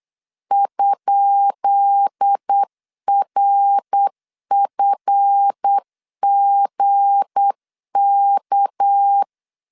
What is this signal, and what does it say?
Morse code, heard as a keyed tone, about 8 wpm, 800 Hz. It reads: ?RFGK
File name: LEZIONI / 10 SIGNS / 10B INT 8wpm 3534